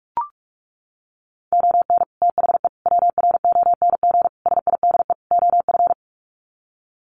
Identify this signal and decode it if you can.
Morse code, heard as a keyed tone, about 44 wpm, 700 Hz. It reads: ON THE WRONG SIDE OF
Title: Sets of 5 Words Encore 45wpm